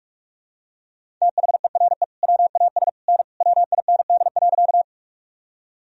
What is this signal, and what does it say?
Morse code, heard as a keyed tone, about 42 wpm, 700 Hz. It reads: THERE WAS N WIND.
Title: Sentences from Top 400 Words and Abbreviations 45wpm